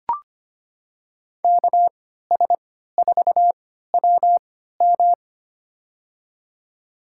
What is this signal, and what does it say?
Morse code, heard as a keyed tone, about 25 wpm, spaced as for 14 wpm, 700 Hz. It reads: KS4WM
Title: USA Call Signs - 5 Characters - 3x Character Spacing - 25wpm